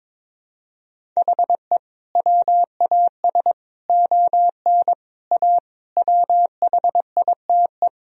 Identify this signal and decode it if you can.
Morse code, heard as a keyed tone, about 22 wpm, 700 Hz. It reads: HE WAS ON A WHITE
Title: Sentences from Top 1000 Words - Review Part 1 - 22wpm